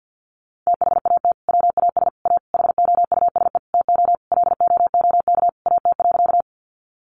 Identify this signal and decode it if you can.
Morse code, heard as a keyed tone, about 50 wpm, 700 Hz. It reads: THAT WAS A HOUSE TO LOOK AT.
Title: Sentences from Top 200 Words 50wpm